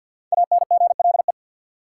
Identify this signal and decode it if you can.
Morse code, heard as a keyed tone, about 51 wpm, 700 Hz. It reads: ANGLE
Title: Five Letter Words - 50wpm